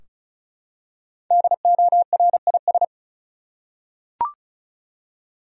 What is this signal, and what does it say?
Morse code, heard as a keyed tone, about 36 wpm, 700 Hz. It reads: DORIS
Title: USA Common Names - Max 5 Letters - 35wpm